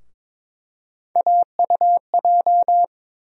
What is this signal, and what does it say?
Morse code, heard as a keyed tone, about 22 wpm, 700 Hz. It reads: AUJ
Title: USA Call Sign Postfixes - 3 Letters 22wpm